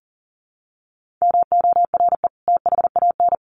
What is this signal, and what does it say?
Morse code, heard as a keyed tone, about 40 wpm, 700 Hz. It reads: MORE THAN
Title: Sets of 2 Words 40wpm